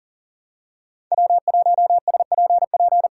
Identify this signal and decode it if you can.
Morse code, heard as a keyed tone, about 40 wpm, 700 Hz. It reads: W1SPP
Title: USA Call Signs - 5 Characters - 2x Character Spacing - 40wpm